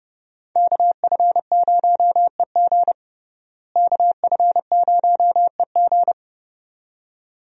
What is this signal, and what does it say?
Morse code, heard as a keyed tone, about 30 wpm, 700 Hz. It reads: KF0EG KF0EG
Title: USA Call Signs - 5 Characters - Repeated 2x - 30wpm